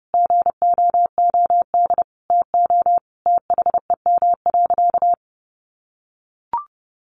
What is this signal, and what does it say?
Morse code, heard as a keyed tone, about 30 wpm, 700 Hz. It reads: GOOD TO THEM.
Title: Sentences from Top 1200 Words 30wpm